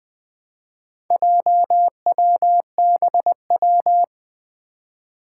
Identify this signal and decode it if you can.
Morse code, heard as a keyed tone, about 20 wpm, 700 Hz. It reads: JWBW
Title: Four Challenging Letter-Number Combinations 20wpm